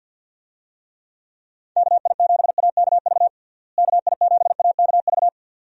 Morse code, heard as a keyed tone, about 51 wpm, 700 Hz. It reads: KI7AKV KI7AKV
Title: USA Call Signs - 6 Characters - Repeated 2x - 50wpm